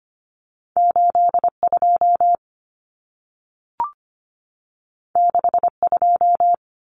Morse code, heard as a keyed tone, about 25 wpm, 700 Hz. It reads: 82 62
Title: Common Cut Numbers - 2-Digit 25wpm